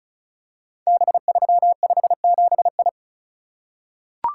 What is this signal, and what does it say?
Morse code, heard as a keyed tone, about 36 wpm, 690 Hz. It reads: B357I
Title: Code Groups - 5 Letter:Numbers - 35wpm